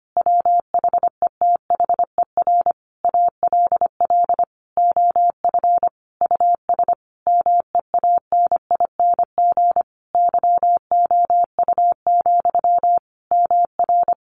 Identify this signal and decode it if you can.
Morse code, heard as a keyed tone, about 25 wpm, 700 Hz. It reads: WHETHER ALL OF US MEANING YOU, MR
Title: A Sound Of Thunder 25wpm